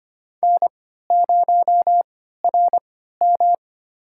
Morse code, heard as a keyed tone, about 25 wpm, spaced as for 14 wpm, 700 Hz. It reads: N0RM
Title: USA Call Signs - 3x Character Spacing - 25wpm